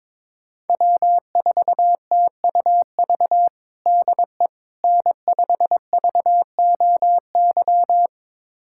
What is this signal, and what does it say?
Morse code, heard as a keyed tone, about 22 wpm, 700 Hz. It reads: W4TUV DE N5VOY
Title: USA Callsign DE Callsign - 5 Characters - 22wpm